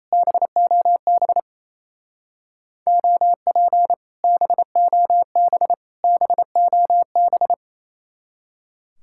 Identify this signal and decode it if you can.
Morse code, heard as a keyed tone, about 34 wpm, 700 Hz. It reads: BOB OP BOB BOB
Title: Speed-Racing Common QSO Phrases Encore - 28wpm